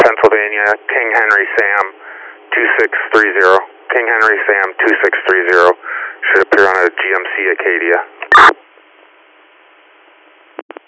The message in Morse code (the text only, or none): none